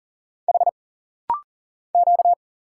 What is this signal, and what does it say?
Morse code, heard as a keyed tone, about 40 wpm, 700 Hz. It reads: H Q
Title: Single Letter-Number - Rapid-Fire 40wpm